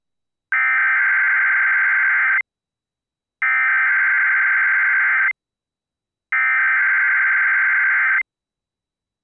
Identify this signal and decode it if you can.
Emergency Alert System (SAME) tones (header bursts), heard as harsh buzzing bursts. It is heard from 0.5 to 8.2 s.